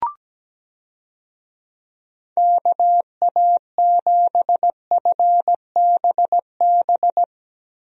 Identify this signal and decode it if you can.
Morse code, heard as a keyed tone, about 17 wpm, 700 Hz. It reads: KA7FBB
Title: USA Call Signs - 6 Characters - Repeated 2x - 17wpm